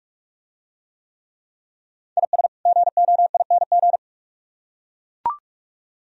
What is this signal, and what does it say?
Morse code, heard as a keyed tone, about 44 wpm, 700 Hz. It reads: IS GOING